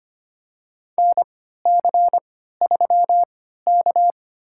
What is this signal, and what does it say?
Morse code, heard as a keyed tone, about 25 wpm, spaced as for 14 wpm, 700 Hz. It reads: NC3K
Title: USA Call Signs - 4 Characters - 3x Character Spacing - 25wpm